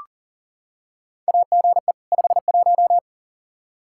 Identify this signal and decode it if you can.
Morse code, heard as a keyed tone, about 40 wpm, 700 Hz. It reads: AGE 51
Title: Practice QSO - Exchange 3 - More Info - 40wpm